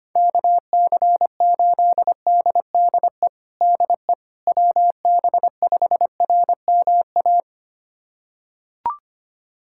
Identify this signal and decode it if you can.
Morse code, heard as a keyed tone, about 25 wpm, 700 Hz. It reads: KC8DDE DE WB5RMA
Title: USA Callsign DE Callsign - 6 Characters - 25wpm